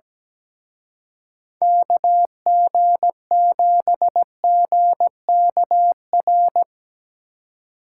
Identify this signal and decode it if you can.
Morse code, heard as a keyed tone, about 18 wpm, 700 Hz. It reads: KG7GKR
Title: Speed Racing - USA 2x3 Call Signs 17wpm